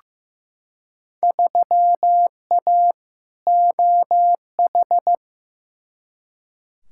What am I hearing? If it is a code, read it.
Morse code, heard as a keyed tone, about 15 wpm, 700 Hz. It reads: 3A OH
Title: Field Day Exchanges - 15wpm